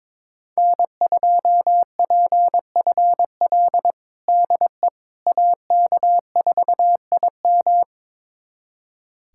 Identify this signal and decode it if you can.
Morse code, heard as a keyed tone, about 22 wpm, 700 Hz. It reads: N2PFL DE AK4IM
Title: USA Callsign DE Callsign - 5 Characters - 22wpm